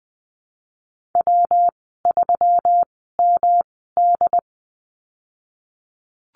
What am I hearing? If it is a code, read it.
Morse code, heard as a keyed tone, about 20 wpm, spaced as for 14 wpm, 700 Hz. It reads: W3MD
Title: 161 Lesson 26g - Letter g - Call Signs 20wpm